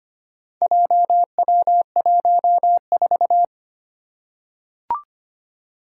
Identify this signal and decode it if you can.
Morse code, heard as a keyed tone, about 25 wpm, 700 Hz. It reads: JW14
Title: Four Challenging Letter-Number Combinations 25wpm